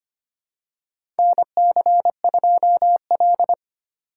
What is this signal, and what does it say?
Morse code, heard as a keyed tone, about 25 wpm, 700 Hz. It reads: NC2L
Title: USA Call Signs - 4 Characters - 6x Character Spacing - 25wpm